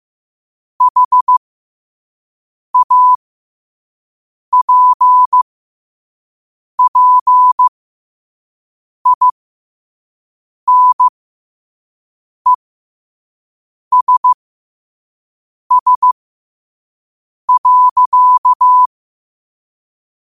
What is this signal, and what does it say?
Morse code, heard as a keyed tone, about 15 wpm, spaced as for 5 wpm, 1000 Hz. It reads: HAPPINESS.
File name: qotdfiles / 5wpm